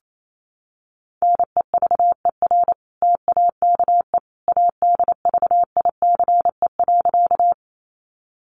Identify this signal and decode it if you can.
Morse code, heard as a keyed tone, about 28 wpm, 700 Hz. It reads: NEVER TAKE ADVICE.
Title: Sentences from Top 1100 Words 28wpm